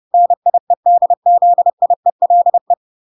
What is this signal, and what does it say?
Morse code, heard as a keyed tone, about 30 wpm, 700 Hz. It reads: NIEDZIELE